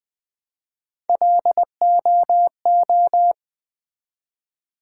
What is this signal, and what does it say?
Morse code, heard as a keyed tone, about 20 wpm, 700 Hz. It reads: LOO
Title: Build A Word - 1 Letter At A Time - 4-Letter Words - 20wpm